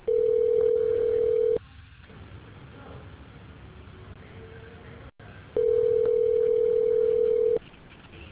Ringback tone (North America), 440 + 480 Hz.